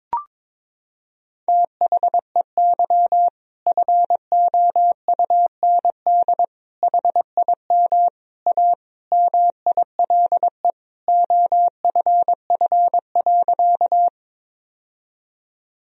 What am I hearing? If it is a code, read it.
Morse code, heard as a keyed tone, about 22 wpm, 700 Hz. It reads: THEY FOUND HIM A MILE OFF.
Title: Sentences from Top 1000 Words - Review Part 1 - 22wpm